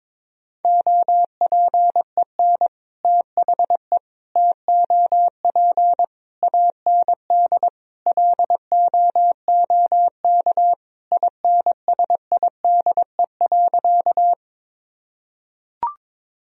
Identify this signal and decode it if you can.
Morse code, heard as a keyed tone, about 22 wpm, 700 Hz. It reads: OPEN THE TOP AND LOOK INSIDE.
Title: Sentences from Top 1000 Words - Review Part 2 - 22wpm